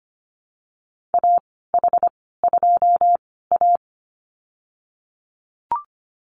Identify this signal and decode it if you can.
Morse code, heard as a keyed tone, about 25 wpm, spaced as for 16 wpm, 700 Hz. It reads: AH2A
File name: Lesson 15g - Number 5 - Call Signs 25wpm